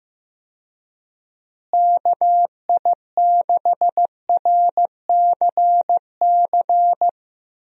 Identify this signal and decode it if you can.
Morse code, heard as a keyed tone, about 15 wpm, 700 Hz. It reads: KI6RCC
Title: USA Call Signs - 6 Characters - Repeated 2x - 15wpm